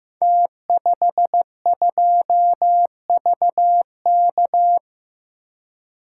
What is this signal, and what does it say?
Morse code, heard as a keyed tone, about 15 wpm, 700 Hz. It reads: T52VK
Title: Code Groups - 5 Letter:Numbers - 15wpm